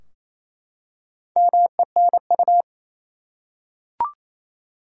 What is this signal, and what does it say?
Morse code, heard as a keyed tone, about 28 wpm, 700 Hz. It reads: MENU
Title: Four Letter Words - 28wpm